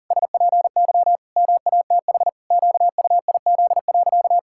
Morse code, heard as a keyed tone, about 40 wpm, 700 Hz. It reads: SPY MATH QUIZ.
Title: Pangram Challenge v2 - 40wpm